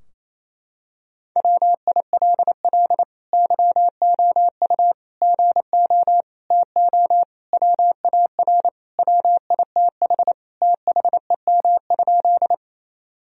Morse code, heard as a keyed tone, about 28 wpm, 700 Hz. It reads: WILL YOU GO TO WAR WITH THEM?
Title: Sentences from Top 1000 Words - Review Part 2 - 28wpm